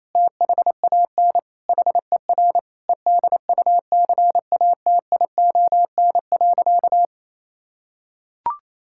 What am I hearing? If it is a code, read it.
Morse code, heard as a keyed tone, about 28 wpm, 700 Hz. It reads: THAN HER EDUCATION.
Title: Sentences from Top 1000 Words - Review Part 6 - 28wpm